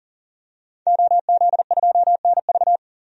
Morse code, heard as a keyed tone, about 40 wpm, 700 Hz. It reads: OZ2NV
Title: DXCC-100 Call Signs - 5 Characters - 40wpm